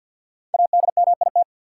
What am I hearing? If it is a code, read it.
Morse code, heard as a keyed tone, about 51 wpm, 700 Hz. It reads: ADMIT
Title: Five Letter Words - 50wpm